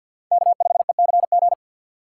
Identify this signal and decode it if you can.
Morse code, heard as a keyed tone, about 50 wpm, 700 Hz. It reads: K5ECG
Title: USA Call Signs - 2x Character Spacing - 50wpm